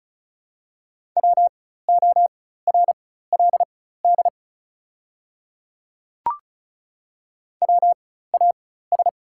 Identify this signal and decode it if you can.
Morse code, heard as a keyed tone, about 36 wpm, spaced as for 16 wpm, 690 Hz. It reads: WORLD WAS
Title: 4x Character Spacing - Top 300 Words - 35wpm